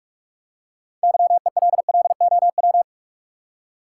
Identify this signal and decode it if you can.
Morse code, heard as a keyed tone, about 46 wpm, 700 Hz. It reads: YELLOW
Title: Six Letter Words - 45wpm